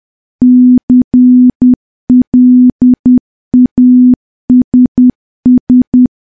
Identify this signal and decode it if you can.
Morse code, heard as a keyed tone, about 10 wpm, 250 Hz. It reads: CLASS
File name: morse / morse-training-10WPM